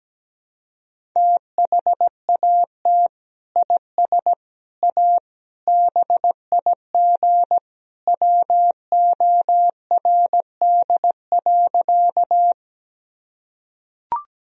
Morse code, heard as a keyed tone, about 17 wpm, 700 Hz. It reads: THAT IS A BIG WORD.
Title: Sentences from Top 1000 Words - Review Part 1 - 17wpm